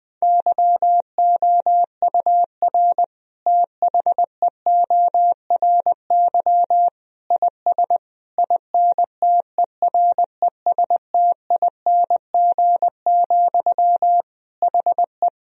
Morse code, heard as a keyed tone, about 21 wpm, 700 Hz. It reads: YOUR THEORY IS INTERESTING, HE
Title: Sentences from Top 1000 Words - Review Part 3 - 20wpm